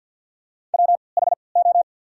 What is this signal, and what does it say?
Morse code, heard as a keyed tone, about 53 wpm, spaced as for 29 wpm, 690 Hz. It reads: WHO
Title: 3x Character Spacing - Top 300 Words - 50wpm